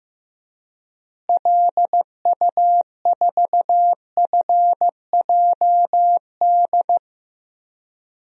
Morse code, heard as a keyed tone, about 15 wpm, 700 Hz. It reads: LU4FJD
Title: DXCC-100 Call Signs - 6 Characters - Repeated 3x - 15wpm